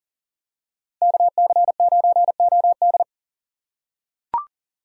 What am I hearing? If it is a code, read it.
Morse code, heard as a keyed tone, about 38 wpm, 700 Hz. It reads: KC9OD
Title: USA Call Signs - 5 Characters - 2x Character Spacing - 40wpm